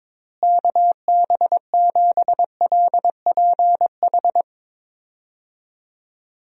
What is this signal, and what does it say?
Morse code, heard as a keyed tone, about 22 wpm, 700 Hz. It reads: KB7LPH
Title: USA Call Signs - 6 Characters - Repeated 3x - 22wpm